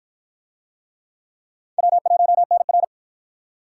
Morse code, heard as a keyed tone, about 53 wpm, 700 Hz. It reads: W1NR